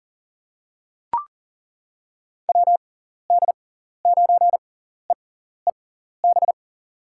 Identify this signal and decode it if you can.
Morse code, heard as a keyed tone, about 38 wpm, spaced as for 14 wpm, 700 Hz. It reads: WD9EEB
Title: USA Call Signs - 6 Characters - 6x Character Spacing - 40wpm